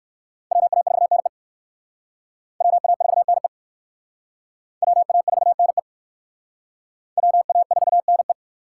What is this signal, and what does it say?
Morse code, heard as a keyed tone, about 56 wpm, 700 Hz. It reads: WA4NE WA4NE WA4NE WA4NE
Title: Speed Racing - USA Call Signs 45wpm